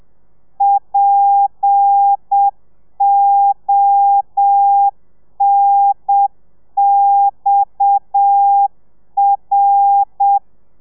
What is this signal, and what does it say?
Morse code, heard as a keyed tone, about 7 wpm, 800 Hz. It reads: PONXR
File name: telegrafia / i2tfj / Lez7 30